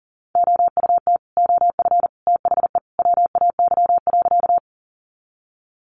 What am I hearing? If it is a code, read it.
Morse code, heard as a keyed tone, about 40 wpm, 700 Hz. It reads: OUT OF THE WAY.
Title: Sentences from Top 100 Words 40wpm